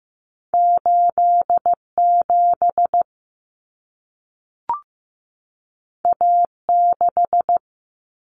Morse code, heard as a keyed tone, about 15 wpm, 700 Hz. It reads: 87 A6
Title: Common Cut Numbers - 2-Digit 15wpm